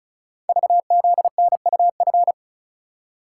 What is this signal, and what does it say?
Morse code, heard as a keyed tone, about 36 wpm, 700 Hz. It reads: VZNUF
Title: Code Groups - 5 Letter:Numbers - 35wpm